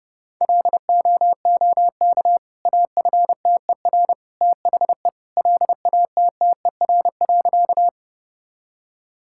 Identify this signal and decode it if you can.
Morse code, heard as a keyed tone, about 32 wpm, 700 Hz. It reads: LOOK AFTER THE LATTER.
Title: Sentences from Top 1600 Words 30wpm